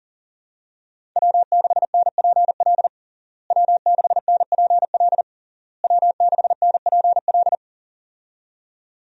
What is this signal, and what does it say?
Morse code, heard as a keyed tone, about 40 wpm, 700 Hz. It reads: W6NPL W6NPL W6NPL
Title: USA Call Signs - 5 Characters - Repeated 3x - 40wpm